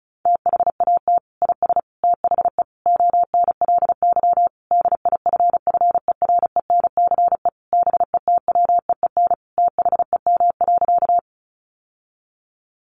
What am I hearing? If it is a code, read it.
Morse code, heard as a keyed tone, about 34 wpm, 700 Hz. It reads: THAT IS THE ONLY DIFFERENCE BETWEEN THEM.
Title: Sentences from Top 800 Words 35wpm